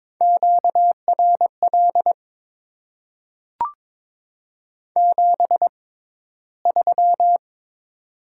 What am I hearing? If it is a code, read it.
Morse code, heard as a keyed tone, about 22 wpm, spaced as for 14 wpm, 700 Hz. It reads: QRL 73
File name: Top 20 QSO Elements - 6x Character Spacing - 22wpm